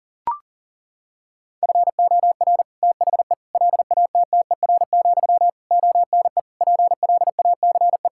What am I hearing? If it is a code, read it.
Morse code, heard as a keyed tone, about 40 wpm, 700 Hz. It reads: FOR THE LATTER, ONE PLACE
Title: Sentences from Top 1600 Words 40wpm